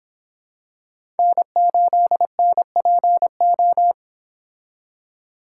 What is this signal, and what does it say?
Morse code, heard as a keyed tone, about 26 wpm, 700 Hz. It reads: N8NPO